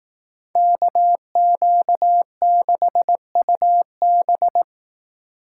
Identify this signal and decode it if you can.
Morse code, heard as a keyed tone, about 18 wpm, 700 Hz. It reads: KQ6UB
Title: Speed Racing - USA 2x2 Call Signs 15wpm